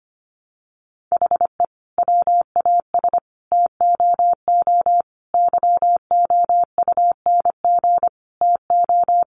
Morse code, heard as a keyed tone, about 25 wpm, 700 Hz. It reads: HE WAS TOO YOUNG TO
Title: Sentences from Top 600 Words 25wpm